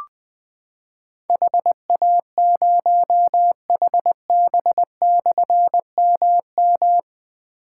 Morse code, heard as a keyed tone, about 20 wpm, 700 Hz. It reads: HA0HB/MM
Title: Difficult Call Signs 20wpm